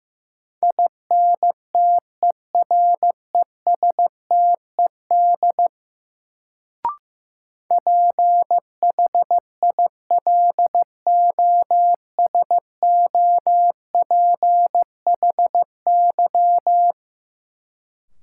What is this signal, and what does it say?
Morse code, heard as a keyed tone, about 15 wpm, 700 Hz. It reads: INTERESTED PHILOSOPHY
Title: Sentences from Top 900 Words - New Word Drill - 15wpm